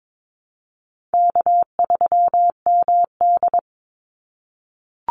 Morse code, heard as a keyed tone, about 22 wpm, 700 Hz. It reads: K3MD
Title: Call Signs - 22wpm